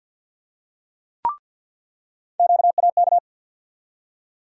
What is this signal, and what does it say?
Morse code, heard as a keyed tone, about 47 wpm, 700 Hz. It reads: QAK
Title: USA Call Sign Postfixes - 3 Letters 50wpm